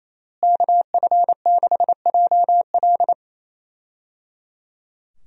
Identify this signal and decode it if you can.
Morse code, heard as a keyed tone, about 28 wpm, 700 Hz. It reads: KF6JL
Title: USA Call Signs - 5 Characters - Repeated 3x - 28wpm